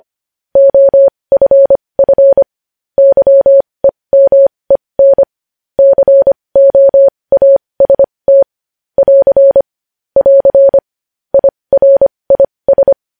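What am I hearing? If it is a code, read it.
Morse code, heard as a keyed tone, about 25 wpm, spaced as for 20 wpm, 550 Hz. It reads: OFF YEMEN COAST <AR> <AR> IRIS
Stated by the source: News headlines sent in Morse code at 20.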